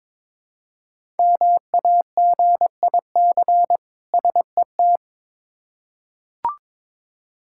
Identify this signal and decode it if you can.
Morse code, heard as a keyed tone, about 22 wpm, 700 Hz. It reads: MAGIC SET
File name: Difficult Words with Hints - 22wpm